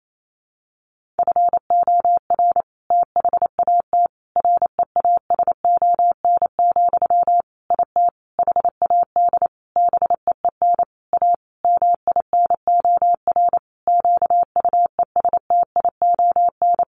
Morse code, heard as a keyed tone, about 28 wpm, 700 Hz. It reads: FOR THAT REASON, IT HAD BEEN A MINOR QUESTION
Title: Sentences from Top 1500 Words 28wpm